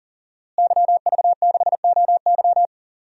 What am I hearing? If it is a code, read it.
Morse code, heard as a keyed tone, about 37 wpm, 700 Hz. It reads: YV6OY
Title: DXCC-100 Call Signs - 5 Characters - Repeated 2x - 40wpm